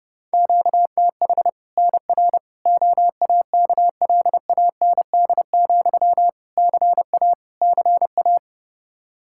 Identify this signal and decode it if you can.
Morse code, heard as a keyed tone, about 30 wpm, 700 Hz. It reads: QTH NR OAKLAND, CA CA
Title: K1USN SST QSO Practice 30wpm